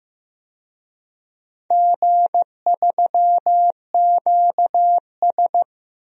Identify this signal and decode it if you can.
Morse code, heard as a keyed tone, about 15 wpm, 700 Hz. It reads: G3QS
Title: DXCC-100 Call Signs - 4 Characters - Repeated 3x - 15wpm